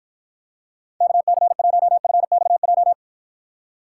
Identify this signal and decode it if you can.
Morse code, heard as a keyed tone, about 54 wpm, 700 Hz. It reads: KC1FXJ